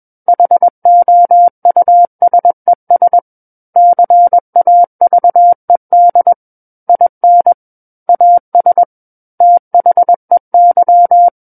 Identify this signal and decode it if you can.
Morse code, heard as a keyed tone, about 21 wpm, 700 Hz. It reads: HOUSES CAVED IN AS THEY
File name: Book One / 12-Chapter 12 - 21 WPM